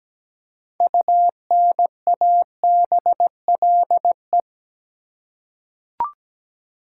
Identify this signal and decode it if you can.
Morse code, heard as a keyed tone, about 17 wpm, 700 Hz. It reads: UNABLE
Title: Six Letter Words - 17wpm